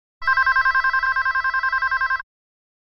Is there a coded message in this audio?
No.